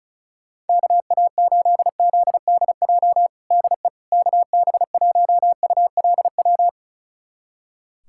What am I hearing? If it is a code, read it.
Morse code, heard as a keyed tone, about 36 wpm, 700 Hz. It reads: KA8ZDJ DE KB1ULW
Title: USA Callsign DE Callsign 35wpm